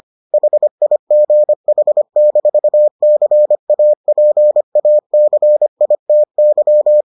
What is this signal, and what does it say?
Morse code, heard as a keyed tone, about 25 wpm, 600 Hz. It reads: HIGH-CAPACITY